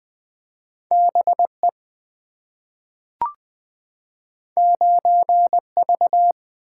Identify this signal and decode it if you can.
Morse code, heard as a keyed tone, about 20 wpm, 700 Hz. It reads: B E 9 V
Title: Two Letter-Number Combinations - 20wpm